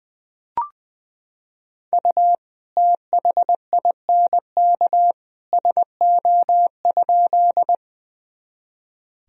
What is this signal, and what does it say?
Morse code, heard as a keyed tone, about 20 wpm, 700 Hz. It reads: U THINK SO?
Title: Sentences from Top 200 Words and Abbreviations 20wpm